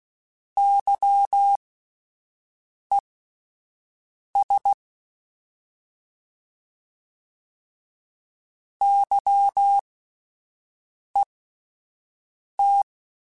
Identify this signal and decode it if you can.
Morse code, heard as a keyed tone, about 16 wpm, spaced as for 5 wpm, 800 Hz. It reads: YES YET
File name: Alphabetical / Alphabetical mp3s / Ys @5